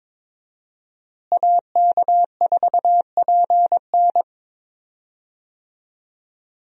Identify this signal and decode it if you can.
Morse code, heard as a keyed tone, about 21 wpm, 700 Hz. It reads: AK4PN
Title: USA Call Signs - 5 Characters - Repeated 2x - 22wpm